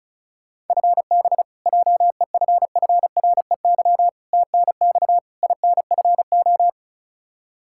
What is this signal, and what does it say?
Morse code, heard as a keyed tone, about 36 wpm, 700 Hz. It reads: FB JEFFREY TNX INFO
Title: Practice QSO - Exchange 3 - More Info - 35wpm